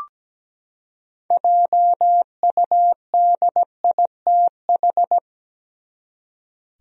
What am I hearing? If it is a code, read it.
Morse code, heard as a keyed tone, about 17 wpm, 700 Hz. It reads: JUDITH
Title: USA Common Names - 17wpm